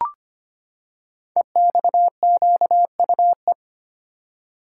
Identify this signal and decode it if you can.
Morse code, heard as a keyed tone, about 25 wpm, 700 Hz. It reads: EXQUE
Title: Code Groups - 5 Letter:Numbers - 25wpm